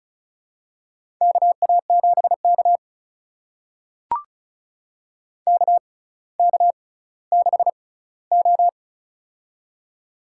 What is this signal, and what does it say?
Morse code, heard as a keyed tone, about 36 wpm, spaced as for 18 wpm, 700 Hz. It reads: KA7K KK6O
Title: USA Call Signs - 4 Characters - 6x Character Spacing - 35wpm